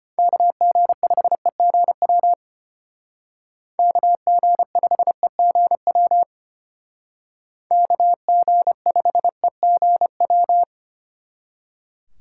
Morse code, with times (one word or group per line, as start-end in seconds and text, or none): 0.2-2.3 KG5EGW
3.8-6.2 KG5EGW
7.7-10.6 KG5EGW